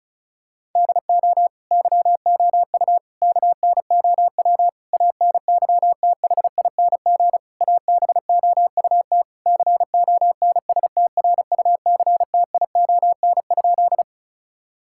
Morse code, heard as a keyed tone, about 36 wpm, 700 Hz. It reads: DO YOU KNOW ANYTHING ABOUT CONSTRUCTION?
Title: Sentences from Top 1600 Words 35wpm